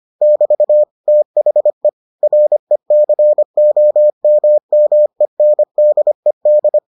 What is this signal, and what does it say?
Morse code, heard as a keyed tone, about 25 wpm, 600 Hz. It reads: <BT> THE RECOMMENDED